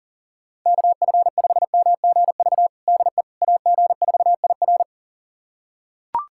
Morse code, heard as a keyed tone, about 40 wpm, 700 Hz. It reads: KF5MGV DE AG4IR
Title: USA Callsign DE Callsign 40wpm